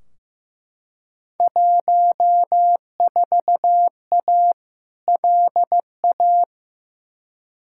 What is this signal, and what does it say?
Morse code, heard as a keyed tone, about 15 wpm, 700 Hz. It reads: 14A LA
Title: Field Day Exchanges Encore - 15wpm